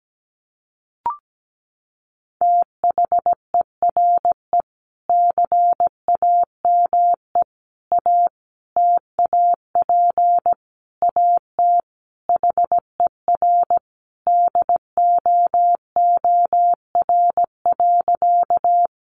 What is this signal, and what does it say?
Morse code, heard as a keyed tone, about 17 wpm, 700 Hz. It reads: THERE CAME A TAP AT HER DOOR.
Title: Sentences from Top 1400 Words 17wpm